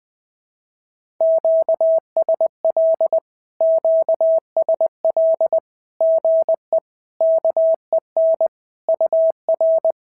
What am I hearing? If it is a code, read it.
Morse code, heard as a keyed tone, about 20 wpm, 650 Hz. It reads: QSL QSL GE KEN UR
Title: SOTA QSO Practice 20wpm